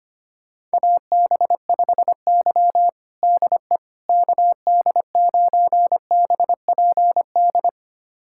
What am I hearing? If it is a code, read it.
Morse code, heard as a keyed tone, about 25 wpm, 700 Hz. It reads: AB5Y DE KD9BPD
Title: USA Callsign DE Callsign 25wpm